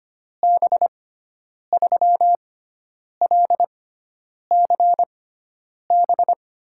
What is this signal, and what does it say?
Morse code, heard as a keyed tone, about 25 wpm, spaced as for 9 wpm, 700 Hz. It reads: B3LCB
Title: USA Call Signs - 6 Characters - 6x Character Spacing - 25wpm